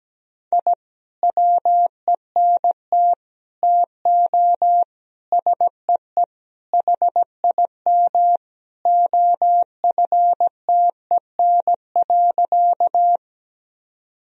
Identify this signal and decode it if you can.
Morse code, heard as a keyed tone, about 17 wpm, 700 Hz. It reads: I WENT TO SEE HIM OFTEN.
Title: Sentences from Top 1000 Words - Review Part 1 - 17wpm